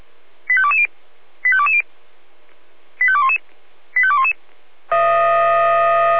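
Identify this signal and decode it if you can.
Five-tone selective call (ZVEI), heard as quick runs of five tones, tones of about 70 ms. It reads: 85220 85220 85211 85211
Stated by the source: ZVEI five-tone selective call 85220, received by SDR.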